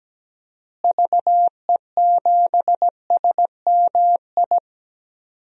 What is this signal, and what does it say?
Morse code, heard as a keyed tone, about 17 wpm, 700 Hz. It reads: VE7SMI
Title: DXCC-100 Call Signs - 6 Characters - Repeated 2x - 17wpm